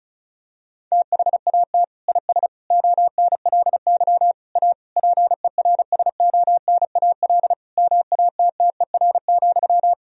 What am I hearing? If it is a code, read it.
Morse code, heard as a keyed tone, about 36 wpm, 700 Hz. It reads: THAT IS ONLY A PERSONAL MATTER,
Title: Sentences from Top 1000 Words - Review Part 4 - 35wpm